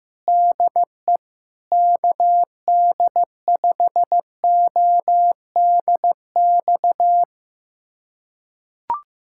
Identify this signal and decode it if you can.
Morse code, heard as a keyed tone, about 15 wpm, 700 Hz. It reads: DE KD5ODX
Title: USA Callsign DE Callsign - 6 Characters - 15wpm